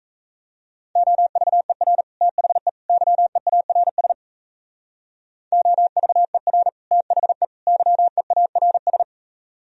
Morse code, heard as a keyed tone, about 34 wpm, 700 Hz. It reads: OVER THE YEARS OVER THE YEARS
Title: Speed-Racing - Sets of 3 Words 28wpm